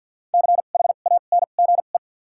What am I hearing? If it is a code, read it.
Morse code, heard as a keyed tone, about 53 wpm, spaced as for 37 wpm, 700 Hz. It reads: CHANGE
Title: 2x Character Spacing - Top 200 Words - 50wpm